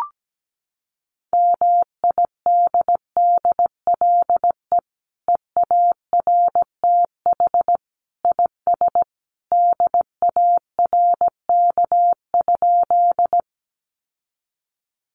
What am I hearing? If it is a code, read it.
Morse code, heard as a keyed tone, about 17 wpm, 700 Hz. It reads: MIDDLE EARTH IS DARK?
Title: Sentences from Top 900 Words 17wpm